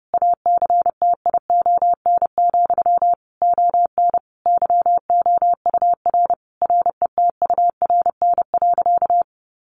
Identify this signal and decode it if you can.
Morse code, heard as a keyed tone, about 30 wpm, 700 Hz. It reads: ACTION, ON YOUR RETURN.
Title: A Sound Of Thunder 30wpm